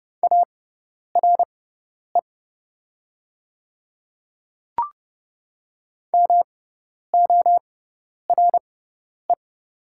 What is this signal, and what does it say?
Morse code, heard as a keyed tone, about 30 wpm, spaced as for 10 wpm, 700 Hz. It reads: ARE MORE
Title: 6x Character Spacing - Top 200 Words - 30wpm